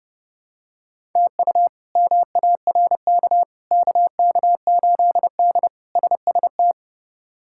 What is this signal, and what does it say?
Morse code, heard as a keyed tone, about 30 wpm, 700 Hz. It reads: TU MARK KK8D SST
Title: K1USN SST QSO Practice - 30wpm